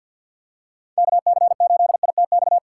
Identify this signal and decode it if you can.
Morse code, heard as a keyed tone, about 47 wpm, 700 Hz. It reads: KC8ITX
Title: USA Call Signs - 6 Characters - 4x Character Spacing - 50wpm